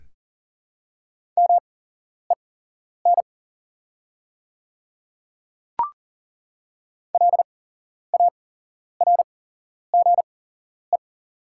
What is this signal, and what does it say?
Morse code, heard as a keyed tone, about 40 wpm, spaced as for 11 wpm, 700 Hz. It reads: MEN LARGE
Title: 8x Character Spacing - Top 200 Words - 40wpm